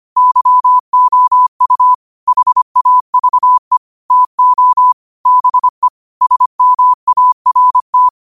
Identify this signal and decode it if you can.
Morse code, heard as a keyed tone, about 25 wpm, 1000 Hz. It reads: YOU HAVE TO BE SMART
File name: qotdfiles / 25wpm